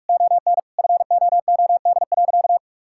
Morse code, heard as a keyed tone, about 41 wpm, 700 Hz. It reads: ON FOOD.
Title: Sentences from Top 1000 Words - Review Part 5 - 45wpm